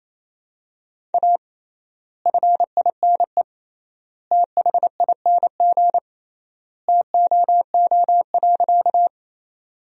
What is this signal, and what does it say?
Morse code, heard as a keyed tone, about 28 wpm, 700 Hz. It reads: A FINE THING TOO.
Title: 3x Word-Spacing - Sentences from Top 500 Words - 28wpm